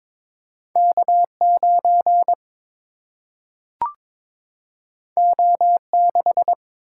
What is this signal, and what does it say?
Morse code, heard as a keyed tone, about 22 wpm, 700 Hz. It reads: K 9 O 6
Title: Single-Letter Single-Number 22wpm